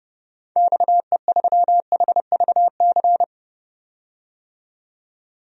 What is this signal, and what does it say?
Morse code, heard as a keyed tone, about 30 wpm, 700 Hz. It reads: XE3HVC
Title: DXCC-100 Call Signs - 6 Characters - 30wpm